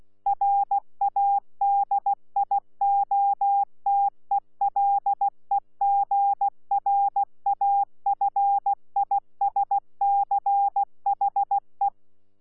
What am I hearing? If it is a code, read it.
Morse code, heard as a keyed tone, about 16 wpm, 800 Hz. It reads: RADIOTELEGRAFISCHE
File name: Z PI4VRZ / PI4VRZ 14-jun-2025